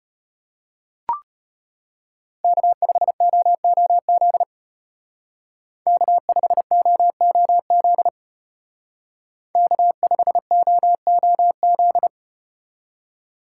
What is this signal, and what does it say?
Morse code, heard as a keyed tone, about 34 wpm, 700 Hz. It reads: K5OOZ K5OOZ K5OOZ